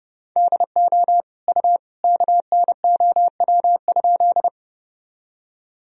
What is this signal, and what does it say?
Morse code, heard as a keyed tone, about 30 wpm, 700 Hz. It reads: DO U KNOW?
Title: Sentences from Top 100 Words and Abbreviations 30wpm